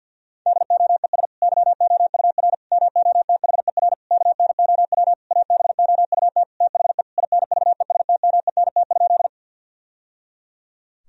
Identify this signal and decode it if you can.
Morse code, heard as a keyed tone, about 50 wpm, 700 Hz. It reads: DOES YOUR MOTHER KNOW ABOUT THE INVESTMENT?
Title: Sentences from Top 1000 Words - Review Part 4 - 50wpm